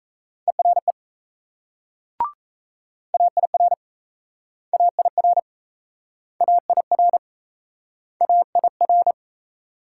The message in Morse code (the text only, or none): ERE AIR AIR AIR AIR